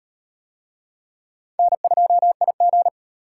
Morse code, heard as a keyed tone, about 38 wpm, 700 Hz. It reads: N2IG